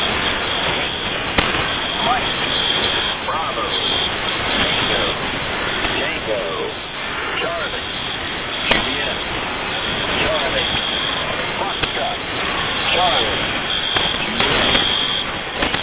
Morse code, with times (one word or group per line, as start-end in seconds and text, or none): none